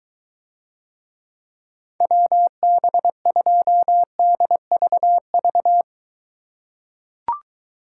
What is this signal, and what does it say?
Morse code, heard as a keyed tone, about 22 wpm, 700 Hz. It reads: WB2DVV